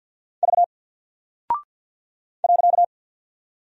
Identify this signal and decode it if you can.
Morse code, heard as a keyed tone, about 48 wpm, 700 Hz. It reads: V .
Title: Single Character (Letter:Number:Punctuation:Prosign) 50wpm